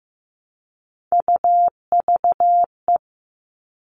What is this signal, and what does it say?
Morse code, heard as a keyed tone, about 15 wpm, 700 Hz. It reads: UVE
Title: All Cut Numbers - 3-Digit 15wpm